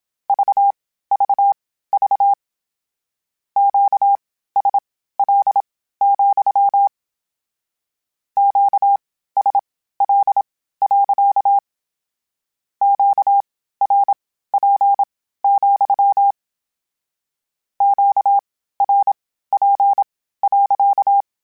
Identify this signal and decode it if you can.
Morse code, heard as a keyed tone, about 27 wpm, spaced as for 15 wpm, 800 Hz. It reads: VVV QSL, QSL. QRP, QRP.
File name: Q Calls / Q Calls-mp3s / Q Calls sppc @25